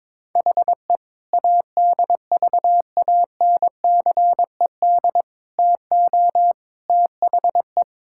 Morse code, heard as a keyed tone, about 22 wpm, 700 Hz. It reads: HE ADVANCED TO THE
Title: Sentences from Top 1600 Words 22wpm